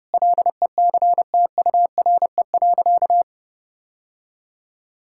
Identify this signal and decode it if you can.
Morse code, heard as a keyed tone, about 30 wpm, 700 Hz. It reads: LECTURE.
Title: Sentences from Top 1600 Words 30wpm